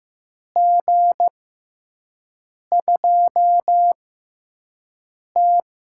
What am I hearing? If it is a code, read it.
Morse code, heard as a keyed tone, about 15 wpm, spaced as for 5 wpm, 700 Hz. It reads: G2T
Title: USA Call Signs - 4 Characters - 6x Character Spacing - 15wpm